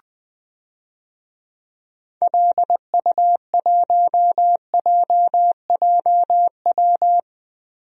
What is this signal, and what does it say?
Morse code, heard as a keyed tone, about 20 wpm, 700 Hz. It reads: LU1JJW
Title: DXCC-100 Call Signs - 6 Characters - Repeated 3x - 20wpm